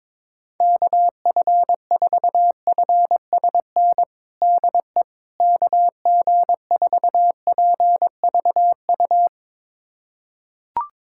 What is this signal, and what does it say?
Morse code, heard as a keyed tone, about 22 wpm, 700 Hz. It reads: KF4FSN DE KG4PVU
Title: USA Callsign DE Callsign - 6 Characters - 22wpm